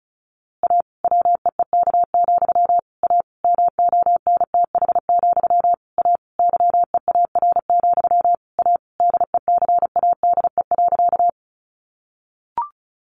Morse code, heard as a keyed tone, about 34 wpm, 700 Hz. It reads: A WEEK, A MONTH, A YEAR, A DECADE.
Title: A Sound Of Thunder 35wpm